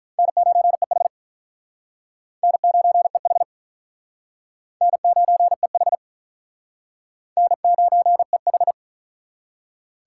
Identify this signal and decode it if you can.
Morse code, heard as a keyed tone, about 42 wpm, 700 Hz. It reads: N9EH N9EH N9EH N9EH
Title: Speed Racing - USA 1x2 Call Signs 35wpm